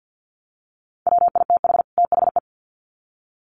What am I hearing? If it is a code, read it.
Morse code, heard as a keyed tone, about 48 wpm, 700 Hz. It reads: WITH THE
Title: Sets of 2 Words 50wpm